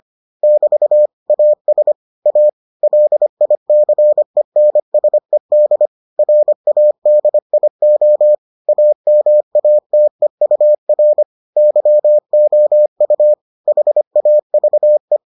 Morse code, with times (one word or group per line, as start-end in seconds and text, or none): 0.4-1.1 <BT>
1.3-1.9 AS
2.3-2.5 A
2.8-5.9 LICENSED
6.2-8.3 RADIO
8.7-11.2 AMATEUR
11.6-13.3 YOU
13.7-15.2 HAVE